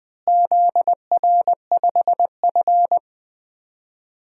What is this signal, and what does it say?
Morse code, heard as a keyed tone, about 20 wpm, 700 Hz. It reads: ZR5F
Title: DXCC-100 Call Signs - 4 Characters - Repeated 3x - 20wpm